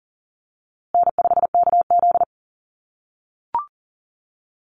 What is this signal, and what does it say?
Morse code, heard as a keyed tone, about 40 wpm, 700 Hz. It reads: N5KZ
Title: USA Call Signs Repeated 3x 40wpm